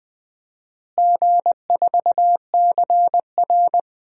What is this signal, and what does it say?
Morse code, heard as a keyed tone, about 21 wpm, 700 Hz. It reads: G4CR
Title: DXCC-100 Call Signs - 4 Characters - Repeated 3x - 20wpm